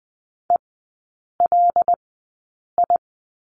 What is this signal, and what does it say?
Morse code, heard as a keyed tone, about 20 wpm, spaced as for 8 wpm, 700 Hz. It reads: ELI
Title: 107 Lesson 18e - Letter m - Three Characters 20wpm